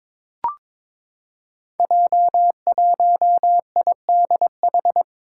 Morse code, heard as a keyed tone, about 22 wpm, 700 Hz. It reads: J1IDH
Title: Five Challenging Letter-Number Combinations 22wpm